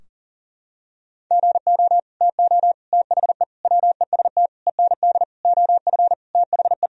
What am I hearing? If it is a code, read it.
Morse code, heard as a keyed tone, about 37 wpm, 700 Hz. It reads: GO TO THE WEST END OF THE
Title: Sentences from Top 1000 Words - Review Part 2 - 40wpm